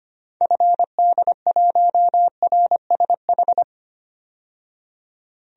Morse code, heard as a keyed tone, about 25 wpm, 690 Hz. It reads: FD1RSH
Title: DXCC-100 Call Signs - 6 Characters - 25wpm